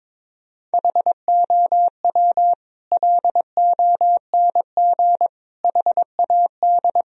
Morse code, heard as a keyed tone, about 22 wpm, 700 Hz. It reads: HOW LONG HAD
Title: Sentences from Top 1000 Words - Review Part 3 - 22wpm